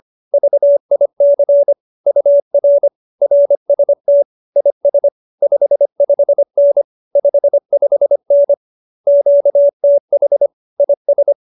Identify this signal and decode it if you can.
Morse code, heard as a keyed tone, about 25 wpm, 570 Hz. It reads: VIC UR RST IS 55N 55N QTH IS